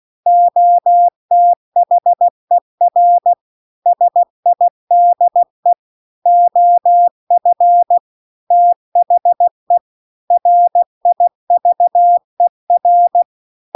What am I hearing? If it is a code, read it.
Morse code, heard as a keyed tone, about 16 wpm, 700 Hz. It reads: OTHER SIDE OF THE RIVER
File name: CW-The-Big-Twoheaded-River-2-16WPM